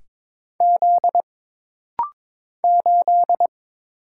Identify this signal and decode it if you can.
Morse code, heard as a keyed tone, about 22 wpm, 700 Hz. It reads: Z 8
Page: This rapid-fire timing is geared to push your ICR (Instant Character Recognition) skills.